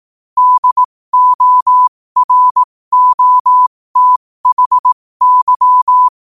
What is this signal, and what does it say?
Morse code, heard as a keyed tone, about 18 wpm, spaced as for 15 wpm, 1000 Hz. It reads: DOROTHY
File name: qotdfiles / 15wpm